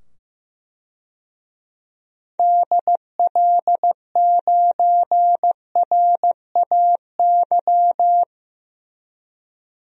Morse code, heard as a keyed tone, about 16 wpm, 700 Hz. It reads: DL9RAY
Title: DXCC-100 Call Signs - 6 Characters - 15wpm